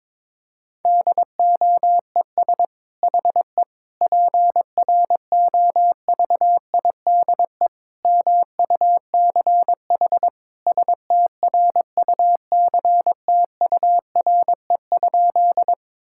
Morse code, heard as a keyed tone, about 22 wpm, 700 Hz. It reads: DOES HE PROVIDE MUCH STRUCTURE?
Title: Sentences from Top 1000 Words - Review Part 6 - 22wpm